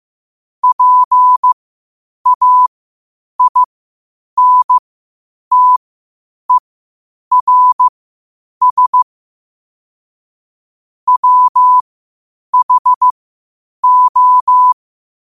Morse code, heard as a keyed tone, about 15 wpm, spaced as for 8 wpm, 1000 Hz. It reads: PAINTERS WHO
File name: qotdfiles / 8wpm